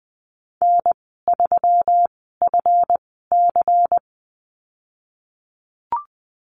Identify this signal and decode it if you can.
Morse code, heard as a keyed tone, about 20 wpm, spaced as for 15 wpm, 700 Hz. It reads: N3FC
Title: 140 Lesson 23g - Letter f - Call Signs 20wpm